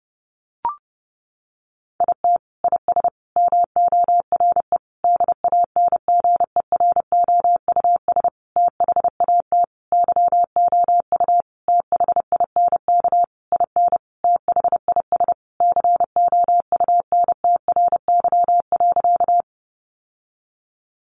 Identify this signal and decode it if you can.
Morse code, heard as a keyed tone, about 30 wpm, 700 Hz. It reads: IT IS MORE DANGEROUS THAT YOU THINK IN THIS COUNTRY.
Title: Sentences from Top 1200 Words 30wpm